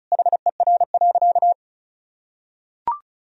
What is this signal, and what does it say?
Morse code, heard as a keyed tone, about 34 wpm, 700 Hz. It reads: HER.
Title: Sentences from Top 1000 Words - Review Part 2 - 35wpm